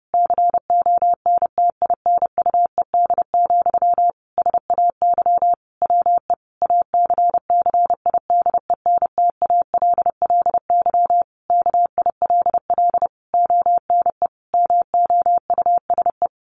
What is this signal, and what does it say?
Morse code, heard as a keyed tone, about 30 wpm, 700 Hz. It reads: CONTINUED, SAY WE ACCIDENTALLY KILL ONE MOUSE
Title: A Sound Of Thunder 30wpm